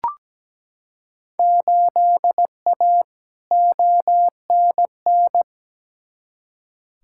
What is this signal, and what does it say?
Morse code, heard as a keyed tone, about 17 wpm, 700 Hz. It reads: 8A ONN
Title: Field Day Exchanges Encore - 17wpm